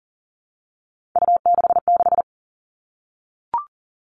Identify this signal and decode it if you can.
Morse code, heard as a keyed tone, about 40 wpm, 700 Hz. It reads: U66
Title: Three Challenging Letter-Number Combinations 40wpm